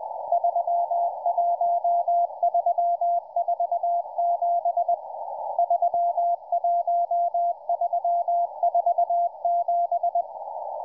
Morse code, heard as a keyed tone, about 20 wpm, 690 Hz, with radio noise. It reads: 31347 31347